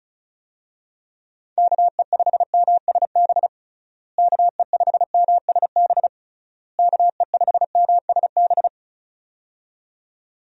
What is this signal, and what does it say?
Morse code, heard as a keyed tone, about 34 wpm, 700 Hz. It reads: KE5MSB KE5MSB KE5MSB
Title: USA Call Signs - 6 Characters - Repeated 3x - 35wpm